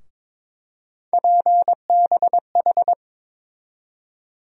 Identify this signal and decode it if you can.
Morse code, heard as a keyed tone, about 21 wpm, 700 Hz. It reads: PBH
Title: USA Call Sign Postfixes - 3 Letters 22wpm